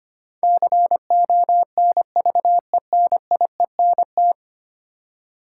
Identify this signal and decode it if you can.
Morse code, heard as a keyed tone, about 25 wpm, 700 Hz. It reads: CONVENIENT
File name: 10 Letter Words - First 3 Letters Given - 25wpm